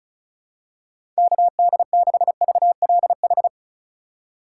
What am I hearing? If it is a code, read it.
Morse code, heard as a keyed tone, about 34 wpm, 700 Hz. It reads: KD6VLH
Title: USA Call Signs - 6 Characters - 2x Character Spacing - 35wpm